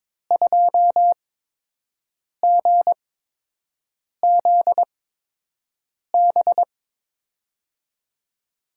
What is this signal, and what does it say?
Morse code, heard as a keyed tone, about 22 wpm, spaced as for 6 wpm, 700 Hz. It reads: 2GZB
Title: USA Call Signs - 6 Characters - 8x Character Spacing - 22wpm